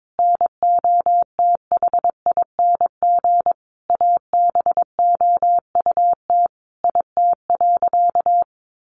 Morse code, heard as a keyed tone, about 22 wpm, 700 Hz. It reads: NOTHING ABOUT IT.
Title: Sentences from Top 500 Words 22wpm